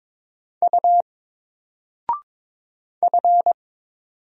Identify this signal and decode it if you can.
Morse code, heard as a keyed tone, about 22 wpm, 700 Hz. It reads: U F
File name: Single-Letter-Number-22wpm